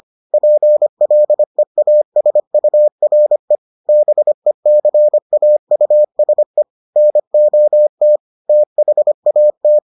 Morse code, heard as a keyed tone, about 25 wpm, 600 Hz. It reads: PLEASURE BECAUSE NOT THAT